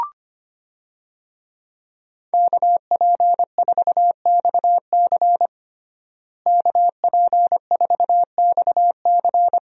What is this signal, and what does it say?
Morse code, heard as a keyed tone, about 25 wpm, 700 Hz. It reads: KP4XC KP4XC
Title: DXCC-100 Call Signs - 5 Characters - Repeated 3x - 25wpm